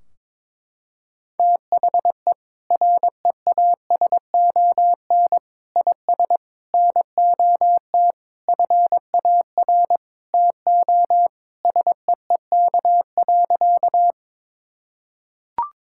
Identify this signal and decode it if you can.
Morse code, heard as a keyed tone, about 22 wpm, 700 Hz. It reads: THE REASON IS NOT FAR TO SEEK.
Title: Sentences from Top 1600 Words 22wpm